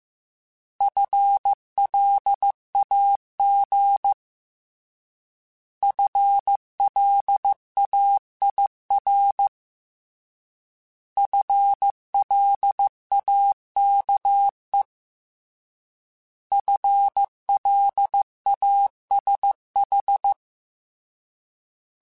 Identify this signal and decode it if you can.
Morse code, heard as a keyed tone, about 15 wpm, 800 Hz. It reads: FLAG FLAIR FLAKE FLASH
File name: Alphabetical / Alphabetical mp3s / Fs @13